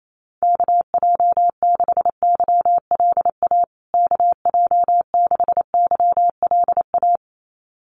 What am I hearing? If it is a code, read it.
Morse code, heard as a keyed tone, about 28 wpm, 700 Hz. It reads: KJ6YLA KJ6YLA
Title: USA Call Signs Repeated 2x 28wpm